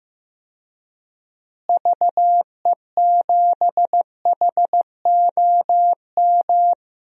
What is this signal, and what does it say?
Morse code, heard as a keyed tone, about 15 wpm, 700 Hz. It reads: VE7HOM
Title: DXCC-100 Call Signs - 6 Characters - Repeated 3x - 15wpm